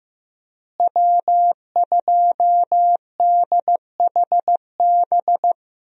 Morse code, heard as a keyed tone, about 15 wpm, 700 Hz. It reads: W2DHB
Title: USA Call Signs - 5 Characters - Repeated 3x - 15wpm